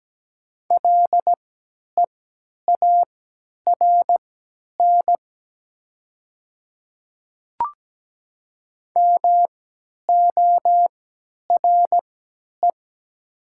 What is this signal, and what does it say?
Morse code, heard as a keyed tone, about 17 wpm, spaced as for 10 wpm, 700 Hz. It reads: LEARN MORE
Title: 3x Character Spacing - Top 500 Words - 17wpm